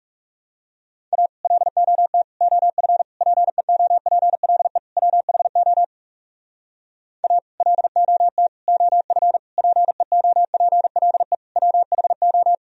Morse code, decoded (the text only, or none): A LOT OF PEOPLE WHO A LOT OF PEOPLE WHO